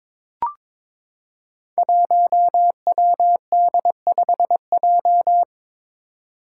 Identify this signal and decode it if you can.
Morse code, heard as a keyed tone, about 22 wpm, 700 Hz. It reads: 1WD5J
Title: Five Challenging Letter-Number Combinations 22wpm